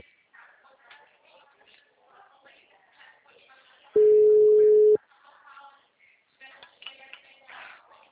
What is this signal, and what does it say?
Ringback tone (Europe, 425 Hz), with speech.
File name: out- -2009- - - .6515